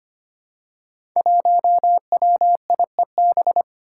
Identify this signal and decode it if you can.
Morse code, heard as a keyed tone, about 25 wpm, 700 Hz. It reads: 1WIEB
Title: Five Challenging Letter-Number Combinations 25wpm